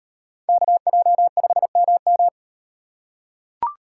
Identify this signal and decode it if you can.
Morse code, heard as a keyed tone, about 38 wpm, 700 Hz. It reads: KJ5MM